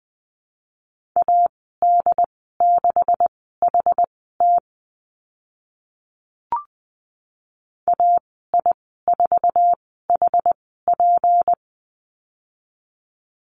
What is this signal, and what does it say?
Morse code, heard as a keyed tone, about 20 wpm, spaced as for 14 wpm, 700 Hz. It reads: AD6HT AI4HP
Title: 161 Lesson 26g - Letter g - Call Signs 20wpm